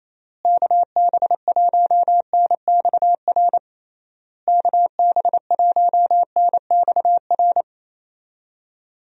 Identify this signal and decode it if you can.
Morse code, heard as a keyed tone, about 28 wpm, 700 Hz. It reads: KB1NXR KB1NXR
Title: USA Call Signs - 6 Characters - Repeated 3x - 28wpm